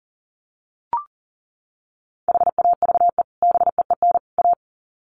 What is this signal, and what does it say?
Morse code, heard as a keyed tone, about 40 wpm, 700 Hz. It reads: HAVE BEEN A
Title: Sets of 3 Words 40wpm